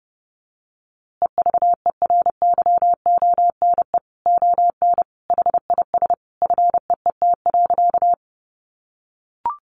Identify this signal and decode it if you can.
Morse code, heard as a keyed tone, about 30 wpm, 700 Hz. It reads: EVERYONE ON HIS FEET.
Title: A Sound Of Thunder 30wpm